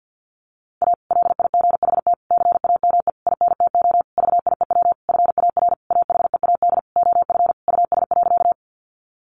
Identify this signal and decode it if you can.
Morse code, heard as a keyed tone, about 50 wpm, 700 Hz. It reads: A LIGHT CAME INTO VIEW FAR AHEAD OF US.
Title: Sentences from Top 1100 Words 50wpm